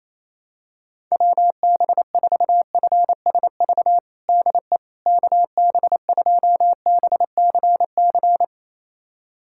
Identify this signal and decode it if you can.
Morse code, heard as a keyed tone, about 28 wpm, 700 Hz. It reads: WB4FSV DE KB2BCC
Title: USA Callsign DE Callsign - 6 Characters - 28wpm